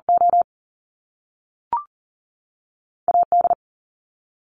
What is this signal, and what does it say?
Morse code, heard as a keyed tone, about 40 wpm, spaced as for 3 wpm, 700 Hz. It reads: OA D
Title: Top Two Letter Combinations in Words 40wpm